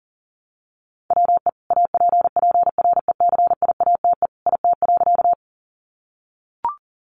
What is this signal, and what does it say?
Morse code, heard as a keyed tone, about 40 wpm, 700 Hz. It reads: WE APPRECIATE IT.
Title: Sentences from Top 1300 Words 40wpm